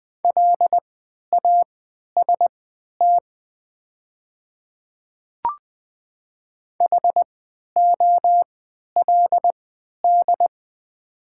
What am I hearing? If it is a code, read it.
Morse code, heard as a keyed tone, about 20 wpm, spaced as for 11 wpm, 690 Hz. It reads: LAST HOLD
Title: 3x Character Spacing - Top 500 Words - 20wpm